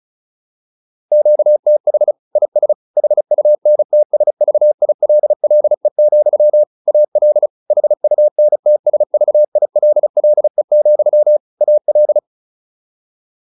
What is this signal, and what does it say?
Morse code, heard as a keyed tone, about 34 wpm, 600 Hz. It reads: QTH IS HUNTSVILLE, AL HUNTSVILLE, AL